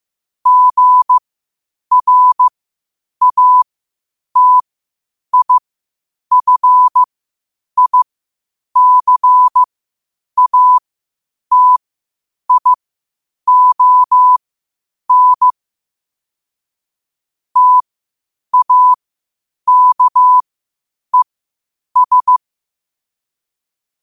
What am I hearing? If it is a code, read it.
Morse code, heard as a keyed tone, about 15 wpm, spaced as for 8 wpm, 1000 Hz. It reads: GRATIFICATION TAKES
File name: qotdfiles / 8wpm